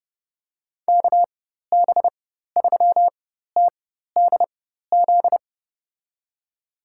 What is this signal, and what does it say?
Morse code, heard as a keyed tone, about 32 wpm, spaced as for 14 wpm, 700 Hz. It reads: KB3TDZ
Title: USA Call Signs - 6 Characters - 4x Character Spacing - 30wpm